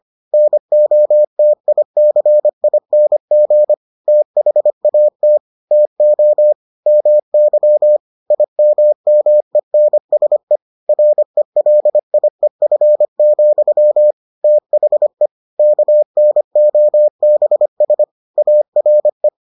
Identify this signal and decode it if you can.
Morse code, heard as a keyed tone, about 25 wpm, 600 Hz. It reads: NOTICING THAT TO MY IMMENSE RELIEF, THE KNOBS ARE